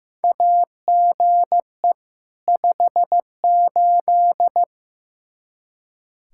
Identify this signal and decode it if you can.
Morse code, heard as a keyed tone, about 15 wpm, 700 Hz. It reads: AGE 58
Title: Practice QSO - Exchange 3 - More Info - 15wpm